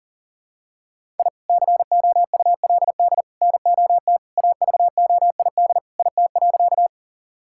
Morse code, heard as a keyed tone, about 40 wpm, 700 Hz. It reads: I COULD NOT AVOID IT.
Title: Sentences from Top 1000 Words - Review Part 6 - 40wpm